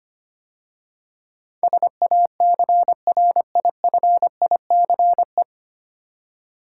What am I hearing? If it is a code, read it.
Morse code, heard as a keyed tone, about 25 wpm, 690 Hz. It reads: SACRIFICE
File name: practice / 25wpm / 9 Letter Words - First 3 Letters Given - 25wpm